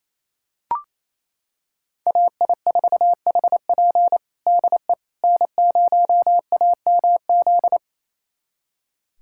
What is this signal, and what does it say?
Morse code, heard as a keyed tone, about 28 wpm, 700 Hz. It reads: AI4HP DE N0AMZ
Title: USA Callsign DE Callsign - 5 Characters - 28wpm